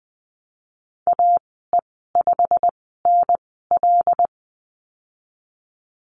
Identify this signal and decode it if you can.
Morse code, heard as a keyed tone, about 20 wpm, spaced as for 15 wpm, 700 Hz. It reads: AE5NL
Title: 128 Lesson 21g - Number 6 - Call Signs 20wpm